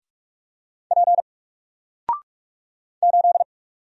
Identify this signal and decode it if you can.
Morse code, heard as a keyed tone, about 46 wpm, 700 Hz. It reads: P 8
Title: Single Letter-Number 45wpm